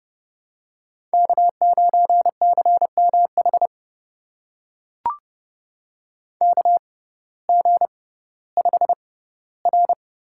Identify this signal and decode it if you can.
Morse code, heard as a keyed tone, about 30 wpm, spaced as for 17 wpm, 700 Hz. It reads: K9CMH KG5R
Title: USA Call Signs - 5 Characters - 6x Character Spacing - 30wpm